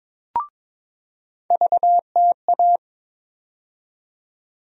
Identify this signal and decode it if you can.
Morse code, heard as a keyed tone, about 22 wpm, 700 Hz. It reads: VTA
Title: USA Call Sign Postfixes - 3 Letters 22wpm